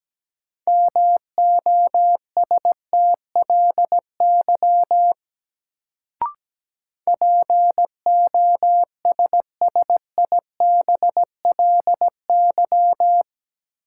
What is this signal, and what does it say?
Morse code, heard as a keyed tone, about 17 wpm, 700 Hz. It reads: MOSTLY POSSIBLY
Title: Sentences from Top 1300 Words - New Word Drill - 17wpm